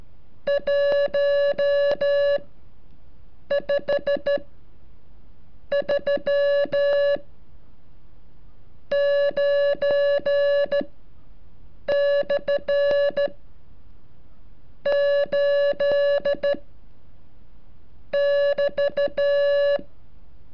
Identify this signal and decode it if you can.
Morse code, heard as a keyed tone, about 13 wpm, spaced as for 6 wpm, 570 Hz. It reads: 1 5 3 9/ 8 <BT>
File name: curso / ejercicio 35